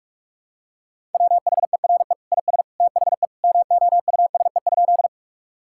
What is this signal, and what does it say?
Morse code, heard as a keyed tone, about 46 wpm, 700 Hz. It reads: WHERE IS THE MOUSE?
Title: Sentences from Top 1000 Words - Review Part 5 - 45wpm